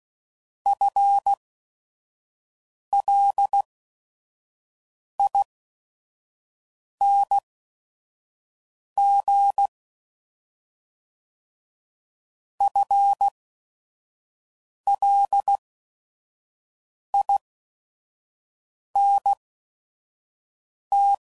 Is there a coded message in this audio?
Yes: Morse code from 0.7 to 21.1 s, heard as a keyed tone, about 16 wpm, spaced as for 5 wpm, 800 Hz. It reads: FLING FLINT